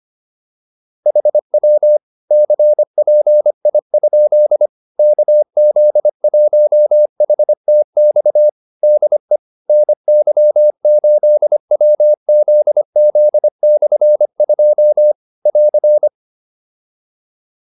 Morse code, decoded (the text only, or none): HW CPI? KZ1HTX DE NY8WZZ/2 <AR>